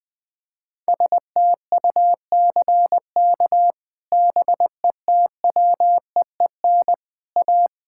Morse code, heard as a keyed tone, about 20 wpm, 700 Hz. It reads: STUCK BETWEEN A
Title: Sets of 8 Words - 20wpm